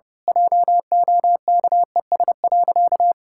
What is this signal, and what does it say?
Morse code, heard as a keyed tone, about 30 wpm, 700 Hz. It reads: JOKES.
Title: Pangram Challenge v2 - 30wpm